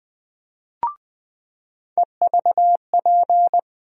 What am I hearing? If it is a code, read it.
Morse code, heard as a keyed tone, about 20 wpm, 700 Hz. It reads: EVP
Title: USA Call Sign Postfixes - 3 Letters 20wpm